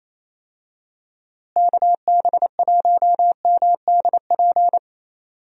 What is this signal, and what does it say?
Morse code, heard as a keyed tone, about 28 wpm, 700 Hz. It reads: KB1MDP
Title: USA Call Signs - 6 Characters - Repeated 2x - 28wpm